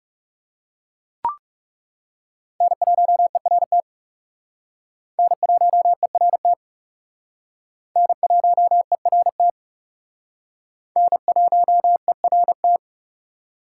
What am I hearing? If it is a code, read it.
Morse code, heard as a keyed tone, about 35 wpm, 700 Hz. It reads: N1ERT N1ERT N1ERT N1ERT
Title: Speed Racing - USA Call Signs 30wpm